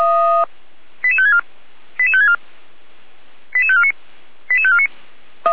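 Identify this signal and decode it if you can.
Five-tone selective call (ZVEI), heard as quick runs of five tones, tones of about 80 ms. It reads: 88564 88564 88549 88549
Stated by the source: ZVEI five-tone call received with an SDR, recorded for Feuerwehr Orlamünde.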